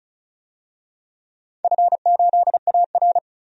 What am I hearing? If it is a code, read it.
Morse code, heard as a keyed tone, about 34 wpm, 700 Hz. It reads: F8AR
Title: DXCC-100 Call Signs - 4 Characters - 35wpm